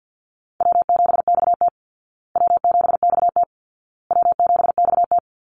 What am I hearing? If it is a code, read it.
Morse code, heard as a keyed tone, about 50 wpm, 700 Hz. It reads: W7XT W7XT W7XT
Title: Call Signs Repeated 2x - 50wpm